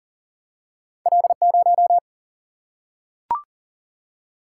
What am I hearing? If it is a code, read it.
Morse code, heard as a keyed tone, about 39 wpm, 700 Hz. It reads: L0
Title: Single-Letter Single-Number 40wpm